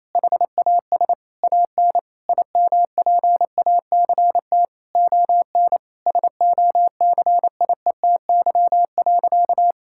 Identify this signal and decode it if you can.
Morse code, heard as a keyed tone, about 28 wpm, 700 Hz. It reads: HAS AN IMPACT ON SOCIETY.
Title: Sentences from Top 1000 Words - Review Part 6 - 28wpm